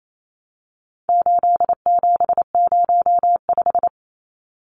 Morse code, heard as a keyed tone, about 28 wpm, 700 Hz. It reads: 8705
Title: Numbers - 4-Digit 28wpm